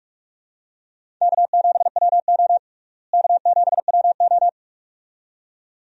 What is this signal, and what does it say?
Morse code, heard as a keyed tone, about 46 wpm, 700 Hz. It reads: K7WO K7WO
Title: USA Call Signs - 4 Characters - Repeated 2x - 45wpm